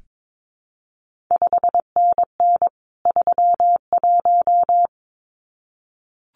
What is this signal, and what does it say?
Morse code, heard as a keyed tone, about 22 wpm, 700 Hz. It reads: 5NN 31
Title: CQ WW DX Contest Exchange - 22wpm